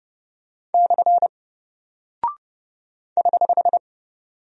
Morse code, heard as a keyed tone, about 30 wpm, 700 Hz. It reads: / <HH>
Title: Single Character (Letter:Number:Punctuation:Prosign) Alt 30wpm